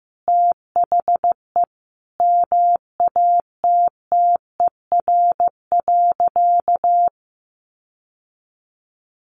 Morse code, heard as a keyed tone, about 15 wpm, 700 Hz. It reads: THE MATTER.
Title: Sentences from Top 1200 Words 15wpm